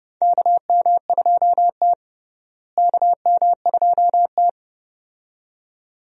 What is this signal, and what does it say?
Morse code, heard as a keyed tone, about 31 wpm, 700 Hz. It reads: KM2T KM2T
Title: USA Call Signs - 4 Characters - Repeated 2x - 30wpm